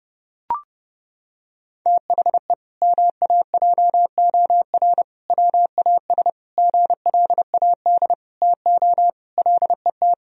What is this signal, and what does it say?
Morse code, heard as a keyed tone, about 30 wpm, 700 Hz. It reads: THE MAJOR WAS GLAD TO LET
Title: Sentences from Top 1000 Words - Review Part 4 - 30wpm